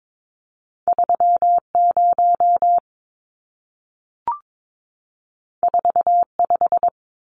Morse code, heard as a keyed tone, about 22 wpm, 700 Hz. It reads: 30 45
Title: Numbers - 2-Digit 22wpm